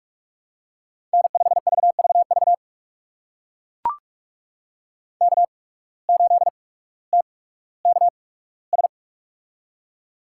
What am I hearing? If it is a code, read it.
Morse code, heard as a keyed tone, about 42 wpm, spaced as for 19 wpm, 700 Hz. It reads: N5VVV K8TKS
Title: USA Call Signs - 5 Characters - 8x Character Spacing - 45wpm